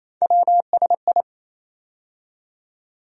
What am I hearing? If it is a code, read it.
Morse code, heard as a keyed tone, about 28 wpm, 700 Hz. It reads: WSI
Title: USA Call Sign Postfixes - 3 Letters 28wpm